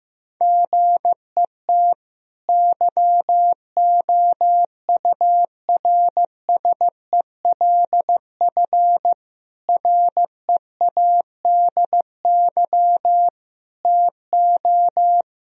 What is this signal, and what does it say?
Morse code, heard as a keyed tone, about 15 wpm, 700 Hz. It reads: GET YOURSELF READY TO
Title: Sentences from Top 1000 Words - Review Part 1 - 15wpm